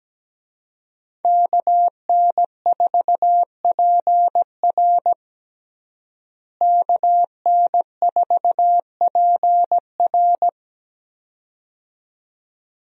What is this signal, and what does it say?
Morse code, heard as a keyed tone, about 17 wpm, 700 Hz. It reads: KN4PR KN4PR
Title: USA Call Signs - 5 Characters - Repeated 2x - 17wpm